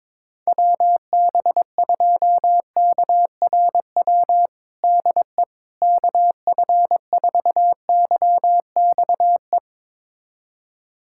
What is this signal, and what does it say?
Morse code, heard as a keyed tone, about 22 wpm, 700 Hz. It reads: WB2KRW DE KF4YXE
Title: USA Callsign DE Callsign - 6 Characters - 22wpm